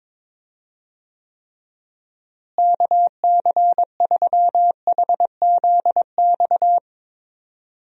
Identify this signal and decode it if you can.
Morse code, heard as a keyed tone, about 22 wpm, 700 Hz. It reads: KC3HZX
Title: USA Call Signs - 6 Characters - Repeated 3x - 22wpm